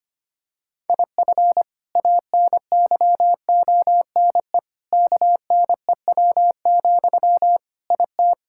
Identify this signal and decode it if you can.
Morse code, heard as a keyed tone, about 25 wpm, 700 Hz. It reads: IF ANYONE KNEW, IT
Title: Sentences from Top 1000 Words - Review Part 6 - 25wpm